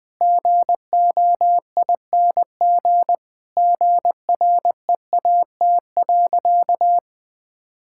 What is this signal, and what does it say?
Morse code, heard as a keyed tone, about 20 wpm, 700 Hz. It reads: GOING GREAT.
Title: Sentences from Top 1000 Words - Review Part 5 - 20wpm